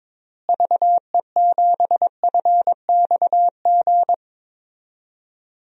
Morse code, heard as a keyed tone, about 22 wpm, 700 Hz. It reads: VE7FXG
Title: DXCC-100 Call Signs - 6 Characters - Repeated 2x - 22wpm